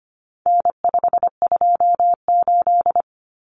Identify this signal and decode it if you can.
Morse code, heard as a keyed tone, about 25 wpm, 700 Hz. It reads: N528
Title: Common Cut Numbers - 4-Digit 25wpm